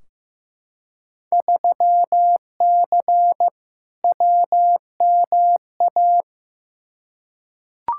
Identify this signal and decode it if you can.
Morse code, heard as a keyed tone, about 15 wpm, 700 Hz. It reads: 3C WMA
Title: Field Day Exchanges Encore - 15wpm